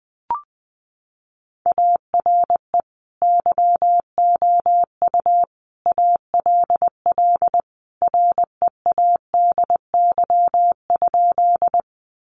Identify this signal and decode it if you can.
Morse code, heard as a keyed tone, about 20 wpm, 700 Hz. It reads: ARE YOU ALL READY?
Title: Sentences from Top 400 Words 20wpm